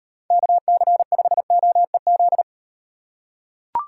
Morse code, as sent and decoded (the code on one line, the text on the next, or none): -.- -.-. ..... --- . --..
KC5OEZ